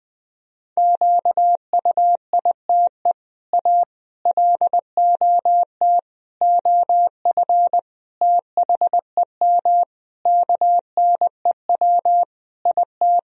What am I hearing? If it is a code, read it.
Morse code, heard as a keyed tone, about 20 wpm, 700 Hz. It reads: QUITE A LOT OF THEM KNEW IT
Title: Sentences from Top 1000 Words - Review Part 3 - 20wpm